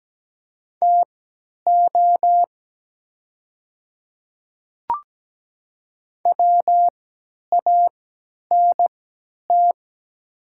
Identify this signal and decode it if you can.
Morse code, heard as a keyed tone, about 17 wpm, spaced as for 10 wpm, 700 Hz. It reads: TO WANT
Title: 3x Character Spacing - Top 200 Words - 17wpm